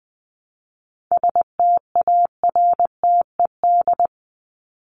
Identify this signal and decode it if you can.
Morse code, heard as a keyed tone, about 20 wpm, 700 Hz. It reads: STARTED
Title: ED Suffix - Top 100 Words with Base Words 20wpm